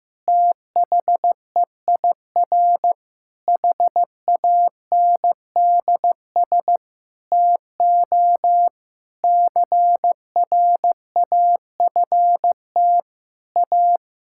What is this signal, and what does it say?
Morse code, heard as a keyed tone, about 16 wpm, 700 Hz. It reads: THEIR HANDS TO CRAFT A
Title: Sentences from Top 1000 Words - Review Part 6 - 15wpm